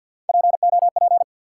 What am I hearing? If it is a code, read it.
Morse code, heard as a keyed tone, about 53 wpm, 700 Hz. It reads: POP
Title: Three Letter Words - 50wpm